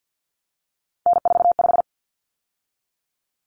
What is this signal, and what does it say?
Morse code, heard as a keyed tone, about 50 wpm, 700 Hz. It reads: N45
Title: Common Cut Numbers - 3-Digit 50wpm